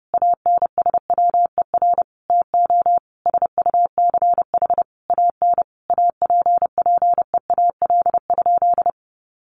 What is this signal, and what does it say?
Morse code, heard as a keyed tone, about 30 wpm, 700 Hz. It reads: ANSWER TO SUCH AN APPEAL?
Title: Sentences from Top 1300 Words 30wpm